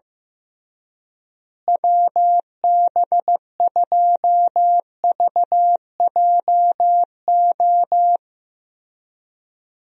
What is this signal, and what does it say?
Morse code, heard as a keyed tone, about 15 wpm, 700 Hz. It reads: WB2VJO
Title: USA Call Signs - 6 Characters - Repeated 2x - 15wpm